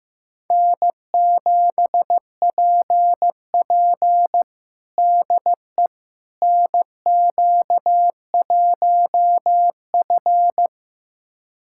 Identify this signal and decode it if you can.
Morse code, heard as a keyed tone, about 15 wpm, 700 Hz. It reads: N7PP DE NQ1F
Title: USA Callsign DE Callsign - 4 Characters - 15wpm